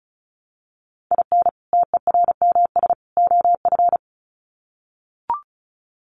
Morse code, heard as a keyed tone, about 34 wpm, 700 Hz. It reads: IN TERMS OF
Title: Sets of 3 Words 35wpm